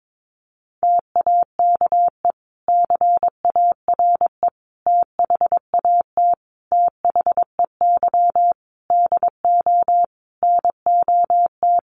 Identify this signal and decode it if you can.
Morse code, heard as a keyed tone, about 22 wpm, 700 Hz. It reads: TAKE CARE THAT THEY DO NOT
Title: Sentences from Top 1500 Words 22wpm